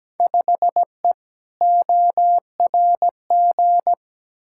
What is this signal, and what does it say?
Morse code, heard as a keyed tone, about 17 wpm, 700 Hz. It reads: 5E ORG
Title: Field Day Exchanges Encore - 17wpm